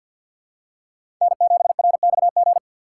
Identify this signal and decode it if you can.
Morse code, heard as a keyed tone, about 51 wpm, 700 Hz. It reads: N7RXG
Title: USA Call Signs - 5 Characters - 4x Character Spacing - 50wpm